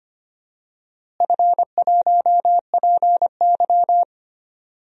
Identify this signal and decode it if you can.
Morse code, heard as a keyed tone, about 25 wpm, 700 Hz. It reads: F1PY
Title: DXCC-100 Call Signs - 4 Characters - Repeated 3x - 25wpm